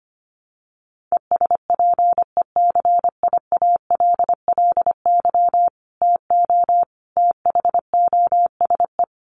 Morse code, heard as a keyed tone, about 25 wpm, 700 Hz. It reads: ESPECIALLY TO THOSE
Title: Sentences from Top 1500 Words 25wpm